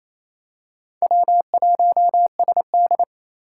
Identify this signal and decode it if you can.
Morse code, heard as a keyed tone, about 27 wpm, 700 Hz. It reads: W1SD
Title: USA Call Signs - 4 Characters - 4x Character Spacing - 28wpm